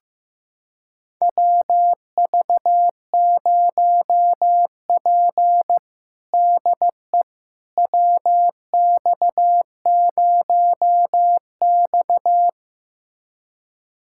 Morse code, heard as a keyed tone, about 16 wpm, 700 Hz. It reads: WV0P DE WX0X
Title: USA Callsign DE Callsign - 4 Characters - 15wpm